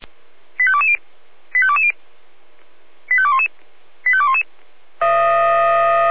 Five-tone selective call (ZVEI), heard as quick runs of five tones, tones of about 70 ms. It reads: 85220 85220 85211 85211
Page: SDR/ZVEI